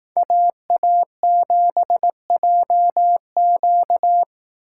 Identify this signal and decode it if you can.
Morse code, heard as a keyed tone, about 19 wpm, 700 Hz. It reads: AA7JQ
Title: Speed Racing - USA 2x2 Call Signs 15wpm